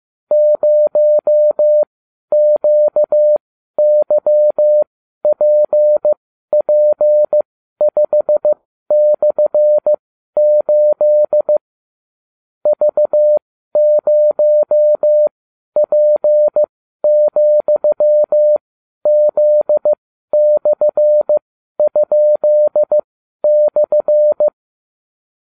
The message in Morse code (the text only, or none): 0QYPP5/8 V0P,Z/?/